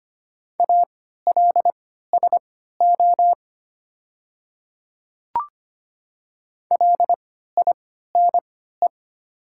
Morse code, heard as a keyed tone, about 25 wpm, spaced as for 14 wpm, 700 Hz. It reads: ALSO LINE
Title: 3x Character Spacing - Top 200 Words - 25wpm